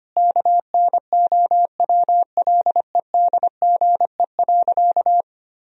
Morse code, heard as a keyed tone, about 25 wpm, 700 Hz. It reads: KNOWLEDGE.
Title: Sentences from Top 1000 Words - Review Part 3 - 25wpm